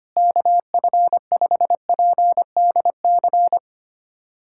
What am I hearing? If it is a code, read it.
Morse code, heard as a keyed tone, about 25 wpm, 700 Hz. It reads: KF5PDC
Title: USA Callsign DE Callsign - 6 Characters - 25wpm